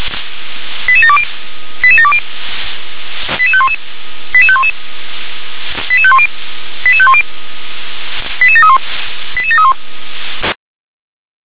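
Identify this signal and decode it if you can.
Five-tone selective call (ZVEI), heard as quick runs of five tones, tones of about 70 ms. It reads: 88720 88720 88511 88511 88510 88510 88731 88731
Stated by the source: ZVEI five-tone call received by SDR.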